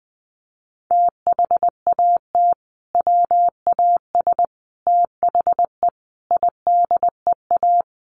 Morse code, heard as a keyed tone, about 20 wpm, 700 Hz. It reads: THAT WAS THE IDEA
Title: Sentences from Top 1100 Words 20wpm